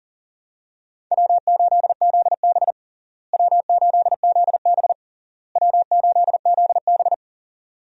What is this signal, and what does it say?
Morse code, heard as a keyed tone, about 40 wpm, 700 Hz. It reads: W8ZB W8ZB W8ZB
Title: USA Call Signs - 4 Characters - Repeated 3x - 40wpm